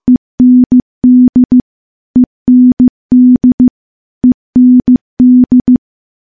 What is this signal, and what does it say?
Morse code, heard as a keyed tone, about 15 wpm, 250 Hz. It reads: END END END
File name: morse-training-15WPM